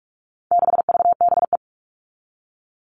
Morse code, heard as a keyed tone, about 44 wpm, 700 Hz. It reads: 6VBE
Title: All Cut Numbers - 4-Digit 45wpm